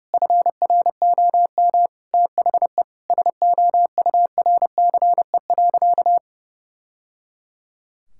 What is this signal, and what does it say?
Morse code, heard as a keyed tone, about 30 wpm, 700 Hz. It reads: FROM THE SOURCE.
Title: Sentences from Top 1000 Words - Review Part 5 - 30wpm